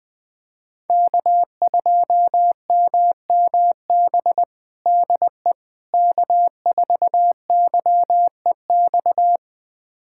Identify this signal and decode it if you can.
Morse code, heard as a keyed tone, about 19 wpm, 700 Hz. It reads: K2MMB DE K4YEX
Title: USA Callsign DE Callsign - 5 Characters - 20wpm